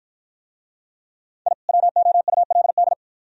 Morse code, heard as a keyed tone, about 51 wpm, 700 Hz. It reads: I WOULD